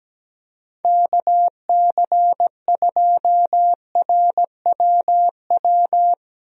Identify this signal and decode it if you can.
Morse code, heard as a keyed tone, about 17 wpm, 700 Hz. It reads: KC2RWW
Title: USA Call Signs - 6 Characters - Repeated 2x - 17wpm